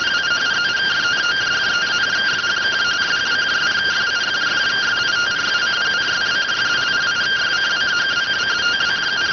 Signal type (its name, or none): two-tone teleprinter signal